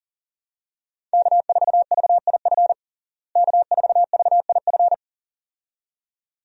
Morse code, heard as a keyed tone, about 40 wpm, 700 Hz. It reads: K4VIF K4VIF
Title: USA Call Signs - 5 Characters - Repeated 2x - 40wpm